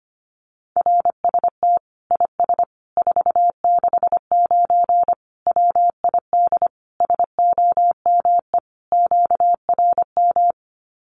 Morse code, heard as a keyed tone, about 25 wpm, 700 Hz. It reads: RST IS 469 WID SOME QRM
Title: Common QSO Phrases 25wpm